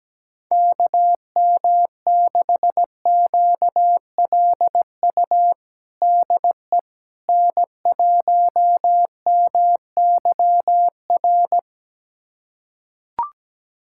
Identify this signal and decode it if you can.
Morse code, heard as a keyed tone, about 17 wpm, 700 Hz. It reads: KM6QLU DE N1MYR
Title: USA Callsign DE Callsign 17wpm